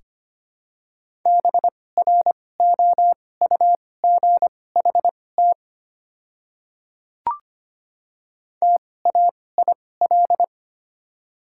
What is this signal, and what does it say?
Morse code, heard as a keyed tone, about 25 wpm, spaced as for 18 wpm, 690 Hz. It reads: BROUGHT TAIL
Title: 2x Character Spacing - Top 500 Words - 25wpm